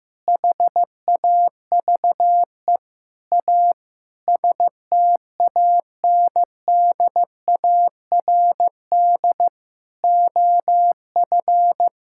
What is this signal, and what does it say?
Morse code, heard as a keyed tone, about 15 wpm, 700 Hz. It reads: HAVE A STANDARD OF
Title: Sentences from Top 1000 Words - Review Part 6 - 15wpm